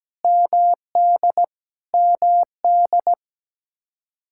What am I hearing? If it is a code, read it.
Morse code, heard as a keyed tone, about 17 wpm, 700 Hz. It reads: MD MD
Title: Practice QSO - Exchange 1 - Beginning - 17wpm